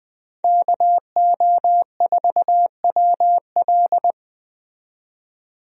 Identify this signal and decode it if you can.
Morse code, heard as a keyed tone, about 20 wpm, 700 Hz. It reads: KO4WL
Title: USA Call Signs - 5 Characters - 20wpm